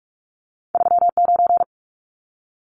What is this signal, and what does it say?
Morse code, heard as a keyed tone, about 46 wpm, 700 Hz. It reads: 39
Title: Numbers - 2-Digit 45wpm